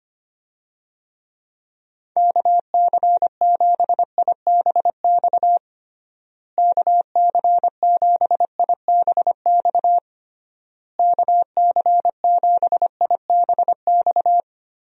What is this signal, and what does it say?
Morse code, heard as a keyed tone, about 25 wpm, 700 Hz. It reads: KC7IBX KC7IBX KC7IBX
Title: USA Call Signs - 6 Characters - Repeated 3x - 25wpm